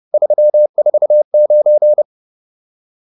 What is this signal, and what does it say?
Morse code, heard as a keyed tone, about 30 wpm, 600 Hz. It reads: 349